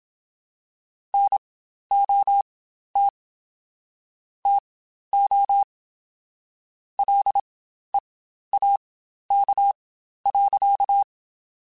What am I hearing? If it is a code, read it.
Morse code, heard as a keyed tone, about 27 wpm, spaced as for 12 wpm, 800 Hz. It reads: NOT TO LEAK.
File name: Quotes Spaced / Quotes Spaced-mp3s / Airtight sp @25